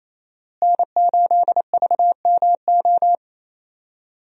Morse code, heard as a keyed tone, about 28 wpm, 700 Hz. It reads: N8VMO
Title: USA Call Signs - 5 Characters - Repeated 3x - 28wpm